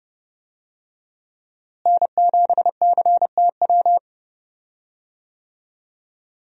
Morse code, heard as a keyed tone, about 30 wpm, 700 Hz. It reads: N7CTW
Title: USA Call Signs - 5 Characters - Repeated 3x - 30wpm